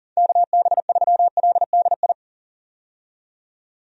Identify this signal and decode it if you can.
Morse code, heard as a keyed tone, about 40 wpm, 700 Hz. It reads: KB3LDI
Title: USA Call Signs - 6 Characters - 40wpm